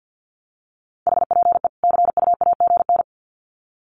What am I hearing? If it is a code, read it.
Morse code, heard as a keyed tone, about 50 wpm, 700 Hz. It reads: HPE CUAGN
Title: Common QSO Phrases 50wpm